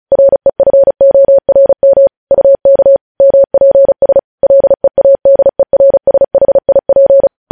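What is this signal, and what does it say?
Morse code, heard as a keyed tone, about 36 wpm, 550 Hz. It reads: REFORM UK MPS LEADERSHIP
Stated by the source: News headlines sent in Morse code at 35 WPM.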